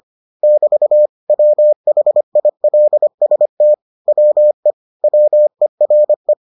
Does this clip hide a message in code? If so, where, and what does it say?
Yes: Morse code from 0.4 to 6.3 s, heard as a keyed tone, about 25 wpm, 600 Hz. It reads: <BT> WHILST WE WERE